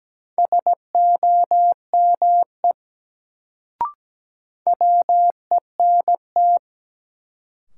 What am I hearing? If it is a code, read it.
Morse code, heard as a keyed tone, about 17 wpm, 700 Hz. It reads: SOME WENT
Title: Sets of 2 Words - New Word Drill - 17wpm